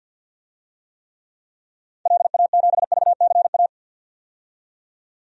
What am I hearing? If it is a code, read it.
Morse code, heard as a keyed tone, about 51 wpm, 700 Hz. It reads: LA7VCA
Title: DXCC-100 Call Signs - 6 Characters - 50wpm